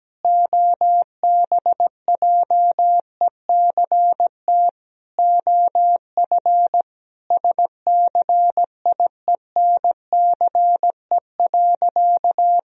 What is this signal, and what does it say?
Morse code, heard as a keyed tone, about 17 wpm, 700 Hz. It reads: OBJECT OF SCIENCE.
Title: Sentences from Top 1000 Words - Review Part 3 - 17wpm